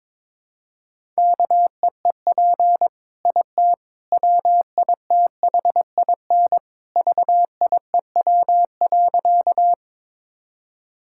Morse code, heard as a keyed tone, about 22 wpm, 700 Hz. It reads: KEEP IT WITHIN VIEW.
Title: Sentences from Top 1000 Words - Review Part 6 - 22wpm